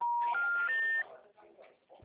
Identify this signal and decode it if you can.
Special information tone, with speech.